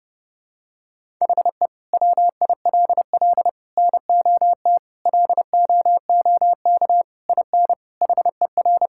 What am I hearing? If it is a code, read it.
Morse code, heard as a keyed tone, about 29 wpm, 700 Hz. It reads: HE WILL NOT LOOK IN HER
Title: Sentences from Top 1000 Words - Review Part 5 - 30wpm